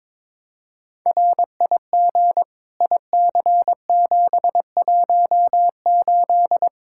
Morse code, heard as a keyed tone, about 22 wpm, 700 Hz. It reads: RIG IC718
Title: Practice QSO - Exchange 2 - Info - 22wpm